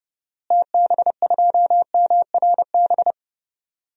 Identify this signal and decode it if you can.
Morse code, heard as a keyed tone, about 30 wpm, 700 Hz. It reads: TB2MRB
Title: DXCC-100 Call Signs - 6 Characters - Repeated 2x - 30wpm